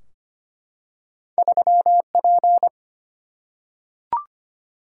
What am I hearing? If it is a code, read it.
Morse code, heard as a keyed tone, about 25 wpm, 700 Hz. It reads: 3P
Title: Single-Letter Single-Number 25wpm